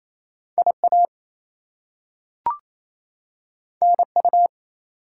Morse code, heard as a keyed tone, about 27 wpm, 700 Hz. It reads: IA NU
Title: US States and Canadian Provinces 28wpm